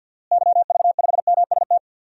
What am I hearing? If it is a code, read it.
Morse code, heard as a keyed tone, about 48 wpm, 700 Hz. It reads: YV5MST
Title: DXCC-100 Call Signs - 6 Characters - Repeated 2x - 50wpm